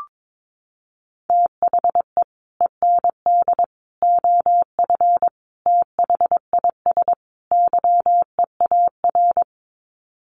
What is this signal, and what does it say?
Morse code, heard as a keyed tone, about 22 wpm, 700 Hz. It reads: THE END OF THIS YEAR
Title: Sets of 5 Words Encore 22wpm